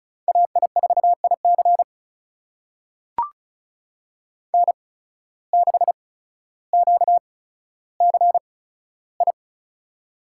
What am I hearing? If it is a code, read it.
Morse code, heard as a keyed tone, about 34 wpm, spaced as for 15 wpm, 700 Hz. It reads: AI4IC N6QCI
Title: USA Call Signs - 5 Characters - 8x Character Spacing - 35wpm